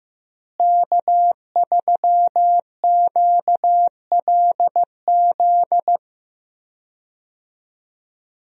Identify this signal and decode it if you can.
Morse code, heard as a keyed tone, about 15 wpm, 700 Hz. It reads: K3QLZ
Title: USA Call Signs - 5 Characters - Repeated 2x - 15wpm